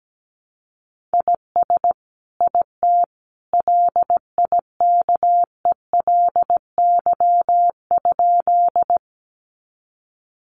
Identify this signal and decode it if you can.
Morse code, heard as a keyed tone, about 17 wpm, 700 Hz. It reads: IS IT LIKELY?
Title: Sentences from Top 600 Words 17wpm